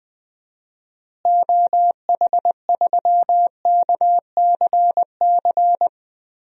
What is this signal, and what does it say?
Morse code, heard as a keyed tone, about 20 wpm, 700 Hz. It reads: OH3KCC
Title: DXCC-100 Call Signs - 6 Characters - Repeated 3x - 20wpm